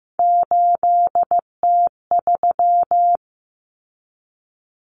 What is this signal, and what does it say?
Morse code, heard as a keyed tone, about 15 wpm, 700 Hz. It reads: 8T3
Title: Common Cut Numbers - 3-Digit 15wpm